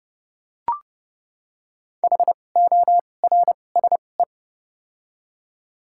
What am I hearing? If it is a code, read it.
Morse code, heard as a keyed tone, about 30 wpm, spaced as for 22 wpm, 700 Hz. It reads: HORSE
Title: 2x Character Spacing - Top 300 Words - 30wpm